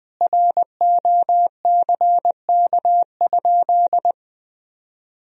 Morse code, heard as a keyed tone, about 21 wpm, 700 Hz. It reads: ROCK?
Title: Sentences from Top 1000 Words - Review Part 2 - 20wpm